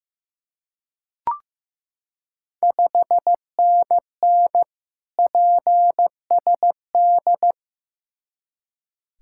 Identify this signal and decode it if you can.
Morse code, heard as a keyed tone, about 15 wpm, 700 Hz. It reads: 5NN PSD
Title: Ontario QSO Party Exchange 15wpm